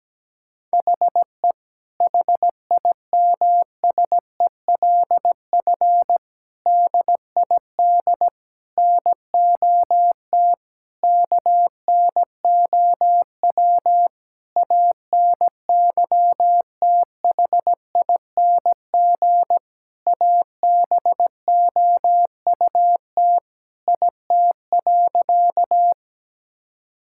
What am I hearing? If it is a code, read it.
Morse code, heard as a keyed tone, about 17 wpm, 700 Hz. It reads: HE HIMSELF DID NOT KNOW ANYTHING ABOUT IT.
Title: Sentences from Top 1600 Words 17wpm